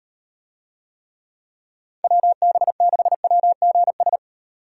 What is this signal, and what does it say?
Morse code, heard as a keyed tone, about 36 wpm, 700 Hz. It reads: WB6WGS